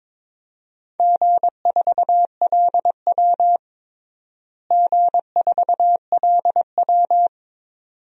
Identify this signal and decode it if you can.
Morse code, heard as a keyed tone, about 22 wpm, 700 Hz. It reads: G4LW G4LW
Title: DXCC-100 Call Signs - 4 Characters - Repeated 2x - 22wpm